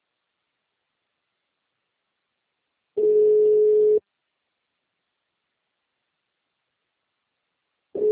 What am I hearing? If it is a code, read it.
Ringback tone (Europe, 425 Hz).
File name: out- -2030